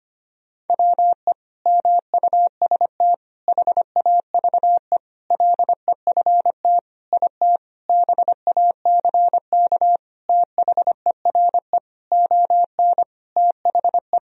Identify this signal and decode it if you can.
Morse code, heard as a keyed tone, about 25 wpm, 700 Hz. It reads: WE MUST HAVE LEFT IT BACK THERE ON THE
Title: Sentences from Top 1000 Words - Review Part 2 - 25wpm